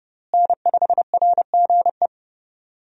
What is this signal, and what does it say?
Morse code, heard as a keyed tone, about 30 wpm, 700 Hz. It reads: N5RGE
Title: USA Call Signs - 5 Characters - Repeated 2x - 30wpm